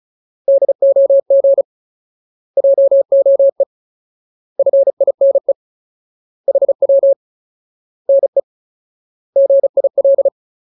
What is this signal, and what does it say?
Morse code, heard as a keyed tone, about 34 wpm, 550 Hz. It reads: DOG JOE FINE HW NE GIL
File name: Ultimate Word List / 35WPM Ultimate